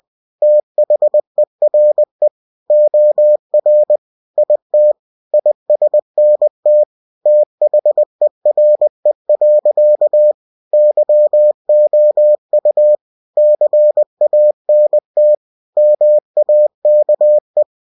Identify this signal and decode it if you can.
Morse code, heard as a keyed tone, about 20 wpm, 600 Hz. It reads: THERE OR IT ISNT THERE. YOU CANT MAKE
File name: cw-books / winnie-the-pooh / winnie-20wpm-page-0027